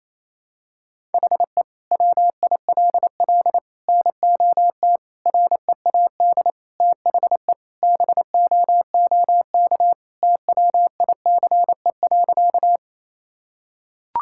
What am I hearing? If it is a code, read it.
Morse code, heard as a keyed tone, about 28 wpm, 700 Hz. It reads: HE WILL NOT READ THE BOOK TWICE.
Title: Sentences from Top 1600 Words 28wpm